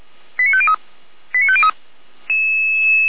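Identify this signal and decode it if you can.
Five-tone selective call (ZVEI), heard as quick runs of five tones, tones of about 70 ms. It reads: 89503 89503
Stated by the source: Five-tone ZVEI selective call received by SDR, feuerwehr-orlamuende.org archive.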